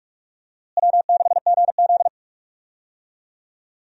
Morse code, heard as a keyed tone, about 46 wpm, 700 Hz. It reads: W6GZ
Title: USA Call Signs - 4 Characters - Repeated 2x - 45wpm